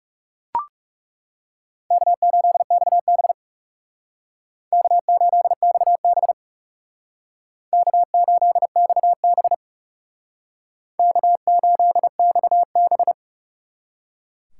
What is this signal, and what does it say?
Morse code, heard as a keyed tone, about 36 wpm, 700 Hz. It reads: K8XB K8XB K8XB K8XB
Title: Speed Racing - USA 1x2 Call Signs 30wpm